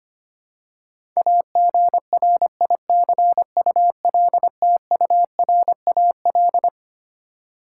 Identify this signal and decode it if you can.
Morse code, heard as a keyed tone, about 25 wpm, 700 Hz. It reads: AGRICULTURAL
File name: Twelve Letter Words - 25wpm